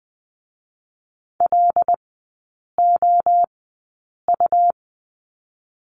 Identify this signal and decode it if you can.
Morse code, heard as a keyed tone, about 20 wpm, 700 Hz. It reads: L O U
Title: 145 Lesson 24e - Letter y - Three Characters 20wpm